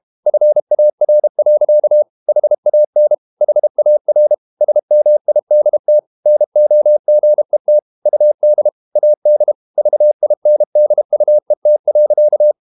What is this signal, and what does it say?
Morse code, heard as a keyed tone, about 32 wpm, 600 Hz. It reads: FAR. HAN HAR SMIDT NOGET UD AD VINDUET.
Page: Ferie 32 wpm